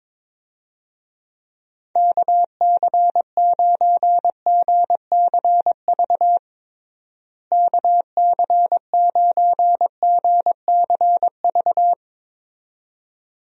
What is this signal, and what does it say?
Morse code, heard as a keyed tone, about 22 wpm, 700 Hz. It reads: KC9GCV KC9GCV
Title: USA Call Signs - 6 Characters - Repeated 2x - 22wpm